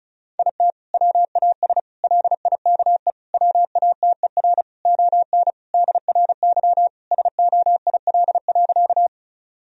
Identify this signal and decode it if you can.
Morse code, heard as a keyed tone, about 36 wpm, 700 Hz. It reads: IT WAS LIKE WATER ON DRY SOIL.
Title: Sentences from Top 1000 Words - Review Part 4 - 35wpm